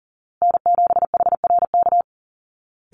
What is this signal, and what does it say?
Morse code, heard as a keyed tone, about 40 wpm, 700 Hz. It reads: N7HRK
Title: Call Signs Repeated 2x - 40wpm